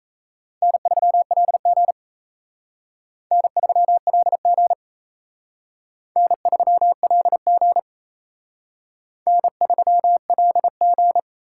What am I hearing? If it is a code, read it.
Morse code, heard as a keyed tone, about 36 wpm, 700 Hz. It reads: N3LG N3LG N3LG N3LG